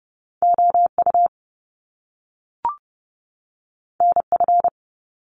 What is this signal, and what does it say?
Morse code, heard as a keyed tone, about 30 wpm, 700 Hz. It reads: OU NF
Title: Top Two Letter Combinations in Words 30wpm